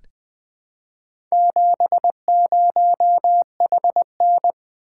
Morse code, heard as a keyed tone, about 20 wpm, 700 Hz. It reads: 70HN
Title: Four Letter-Number Combinations - 20wpm